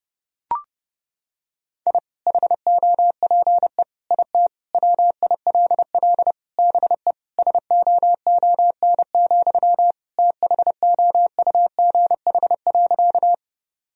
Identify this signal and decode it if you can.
Morse code, heard as a keyed tone, about 32 wpm, 700 Hz. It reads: I HOPE IT WILL BE SOON, THOUGH.
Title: Sentences from Top 1000 Words - Review Part 2 - 30wpm